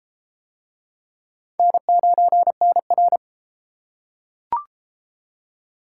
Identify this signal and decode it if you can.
Morse code, heard as a keyed tone, about 32 wpm, 700 Hz. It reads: N9NR